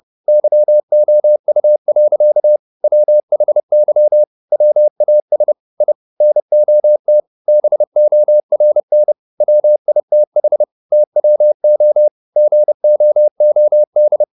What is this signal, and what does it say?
Morse code, heard as a keyed tone, about 30 wpm, 600 Hz. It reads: YOU. WHY WAS I NOT BORN WITH TWO GOOD